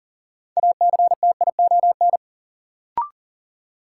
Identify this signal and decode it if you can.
Morse code, heard as a keyed tone, about 40 wpm, 700 Hz. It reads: ACTION
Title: Sentences from Top 700 Words - New Word Drill - 40wpm